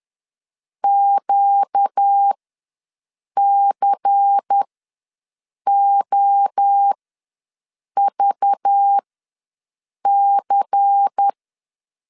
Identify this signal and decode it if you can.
Morse code, heard as a keyed tone, about 10 wpm, spaced as for 6 wpm, 800 Hz. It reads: QCOVC
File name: LEZIONI / 5 LBVC / 5 LBVC ws 10wpm 3733